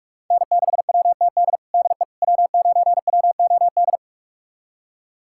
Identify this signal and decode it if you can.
Morse code, heard as a keyed tone, about 41 wpm, 700 Hz. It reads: N6WTD DE W9WOD
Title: USA Callsign DE Callsign - 5 Characters - 45wpm